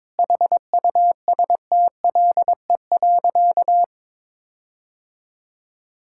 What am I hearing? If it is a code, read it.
Morse code, heard as a keyed tone, about 22 wpm, 700 Hz. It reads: HUSTLE.
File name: Difficult Sentences with Hints - 22wpm